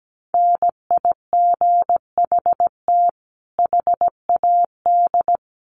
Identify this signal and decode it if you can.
Morse code, heard as a keyed tone, about 17 wpm, 700 Hz. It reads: NIGHT HAD
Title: Sentences from Top 1300 Words 17wpm